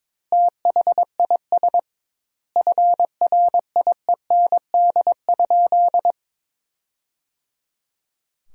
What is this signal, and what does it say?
Morse code, heard as a keyed tone, about 22 wpm, 700 Hz. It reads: THIS FRIEND?
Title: 2x Word-Spacing - Sentences from Top 300 Words - 22wpm